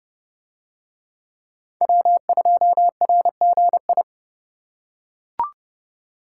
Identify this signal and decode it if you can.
Morse code, heard as a keyed tone, about 31 wpm, 700 Hz. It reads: W2RGI